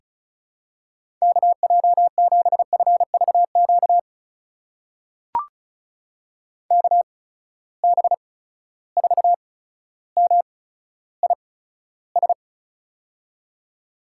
Morse code, heard as a keyed tone, about 36 wpm, spaced as for 15 wpm, 700 Hz. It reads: KJ7FVQ KB4MIS
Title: USA Call Signs - 6 Characters - 8x Character Spacing - 35wpm